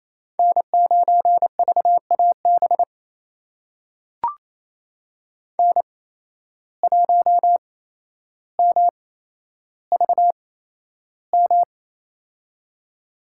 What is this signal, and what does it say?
Morse code, heard as a keyed tone, about 28 wpm, spaced as for 12 wpm, 700 Hz. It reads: N9VAB N1MVM
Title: USA Call Signs - 8x Character Spacing - 28wpm